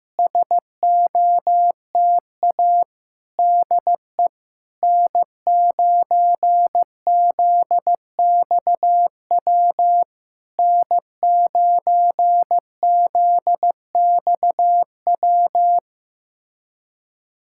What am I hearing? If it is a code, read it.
Morse code, heard as a keyed tone, about 15 wpm, 700 Hz. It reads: SOTA DE N9ZXW N9ZXW
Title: SOTA QSO Practice 15wpm